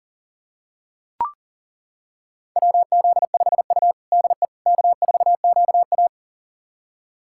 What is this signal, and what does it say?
Morse code, heard as a keyed tone, about 37 wpm, 700 Hz. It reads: WZ5U DE K4QA
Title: USA Callsign DE Callsign - 4 Characters - 40wpm